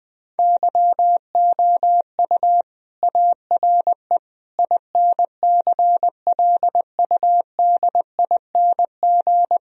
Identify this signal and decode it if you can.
Morse code, heard as a keyed tone, about 20 wpm, 700 Hz. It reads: YOU ARE INCLUDING
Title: Sentences from Top 1000 Words - Review Part 4 - 20wpm